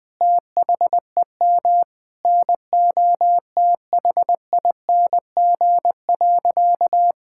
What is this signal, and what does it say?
Morse code, heard as a keyed tone, about 21 wpm, 700 Hz. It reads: THEM NOTHING.
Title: Sentences from Top 1000 Words - Review Part 6 - 20wpm